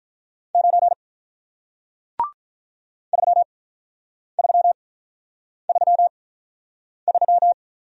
Morse code, heard as a keyed tone, about 48 wpm, spaced as for 9 wpm, 700 Hz. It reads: 9 3333